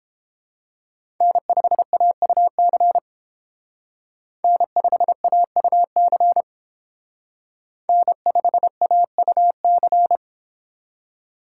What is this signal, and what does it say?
Morse code, heard as a keyed tone, about 28 wpm, 700 Hz. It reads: N5AUC N5AUC N5AUC